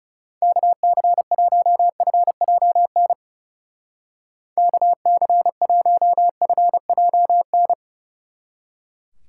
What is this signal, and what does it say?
Morse code, heard as a keyed tone, about 36 wpm, 700 Hz. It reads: KC1FJN KC1FJN
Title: Speed Racing - USA 2x3 Call Signs 30wpm